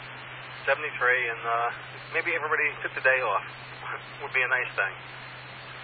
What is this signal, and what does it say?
Emergency Alert System (SAME) tones, heard as harsh buzzing bursts, with speech. It is heard from 0.9 to 5.0 s.